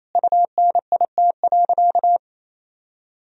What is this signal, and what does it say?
Morse code, heard as a keyed tone, about 28 wpm, 700 Hz. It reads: UNIT.
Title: Sentences from Top 1000 Words - Review Part 2 - 28wpm